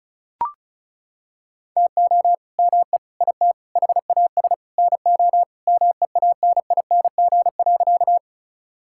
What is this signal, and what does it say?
Morse code, heard as a keyed tone, about 36 wpm, 700 Hz. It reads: TO ME IT HAS NO MEANING.
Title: Sentences from Top 1000 Words - Review Part 6 - 35wpm